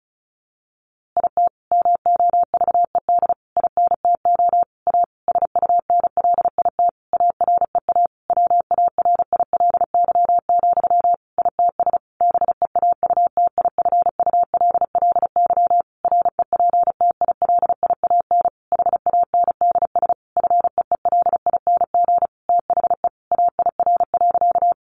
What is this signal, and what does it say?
Morse code, heard as a keyed tone, about 34 wpm, 700 Hz. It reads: IT MOVED INTO A SUNLIT AREA WARILY, ITS BEAUTIFULLY REPTILIAN HANDS FEELING THE AIR.
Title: A Sound Of Thunder 35wpm